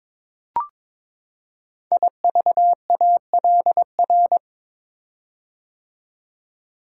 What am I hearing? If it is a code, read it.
Morse code, heard as a keyed tone, about 22 wpm, 700 Hz. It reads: IVALR
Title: Code Groups - 5 Letters - 22wpm